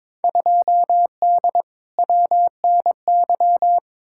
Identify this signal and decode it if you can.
Morse code, heard as a keyed tone, about 22 wpm, 700 Hz. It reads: 2D WNY
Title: Field Day Exchanges - 22wpm